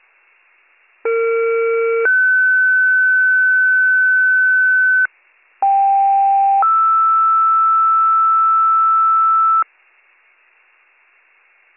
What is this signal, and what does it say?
Two-tone page, heard as a one-second tone, then a three-second tone, about 460 then 1500 Hz, about 790 then 1320 Hz. It is heard from 1.0 to 9.6 s.